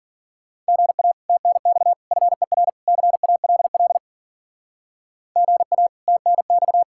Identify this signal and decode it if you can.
Morse code, heard as a keyed tone, about 48 wpm, 700 Hz. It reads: GA TNX FER CALL GA TNX
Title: Speed-Racing Common QSO Phrases - 40wpm